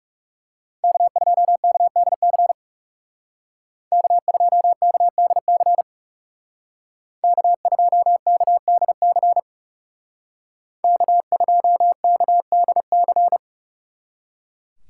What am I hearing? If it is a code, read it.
Morse code, heard as a keyed tone, about 39 wpm, 700 Hz. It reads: K2KDC K2KDC K2KDC K2KDC